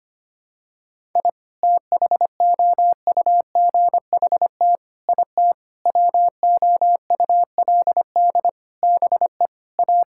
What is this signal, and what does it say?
Morse code, heard as a keyed tone, about 25 wpm, 700 Hz. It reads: I THOUGHT IT WOULD BE A
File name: Sets of 9 Words - 25wpm